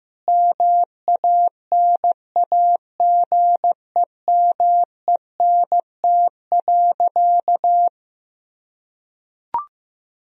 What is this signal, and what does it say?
Morse code, heard as a keyed tone, about 14 wpm, 700 Hz. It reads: MANAGEMENT.
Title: Sentences from Top 1000 Words - Review Part 4 - 15wpm